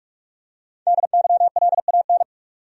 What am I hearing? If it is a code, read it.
Morse code, heard as a keyed tone, about 46 wpm, 700 Hz. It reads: DYLAN
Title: USA Common Names - Max 5 Letters - 45wpm